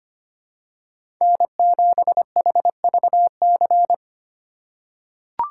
Morse code, heard as a keyed tone, about 25 wpm, 700 Hz. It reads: N7HVC
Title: USA Call Signs - 4x Character Spacing - 25wpm